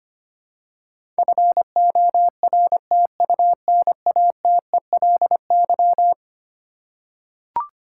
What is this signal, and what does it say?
Morse code, heard as a keyed tone, about 25 wpm, 700 Hz. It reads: FORTUNATELY
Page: morse-code-ninja-practice-25wpm